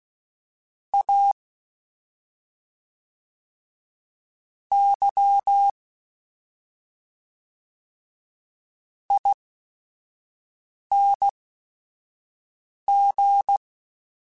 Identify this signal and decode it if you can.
Morse code, heard as a keyed tone, about 16 wpm, spaced as for 5 wpm, 800 Hz. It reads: A Y ING
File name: Special INGs / Special INGs-mp3s / SlowToFaster ing @5